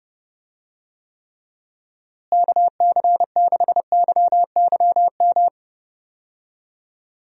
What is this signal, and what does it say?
Morse code, heard as a keyed tone, about 30 wpm, 700 Hz. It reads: KC6YYM
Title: USA Call Signs - 6 Characters - Repeated 2x - 30wpm